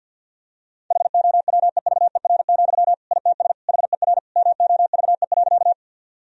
Morse code, heard as a keyed tone, about 50 wpm, 700 Hz. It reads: HOWEVER, ITS HER MOVE.
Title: Sentences from Top 1000 Words - Review Part 1 - 50wpm